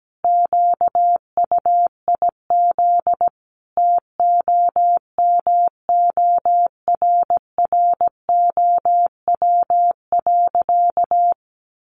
Morse code, heard as a keyed tone, about 17 wpm, 700 Hz. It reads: QUIZ TOMORROW.
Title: Additional Easy Phrases 17wpm